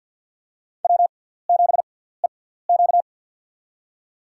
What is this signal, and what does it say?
Morse code, heard as a keyed tone, about 53 wpm, spaced as for 17 wpm, 700 Hz. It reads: W7EQ
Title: USA Call Signs - 4 Characters - 6x Character Spacing - 50wpm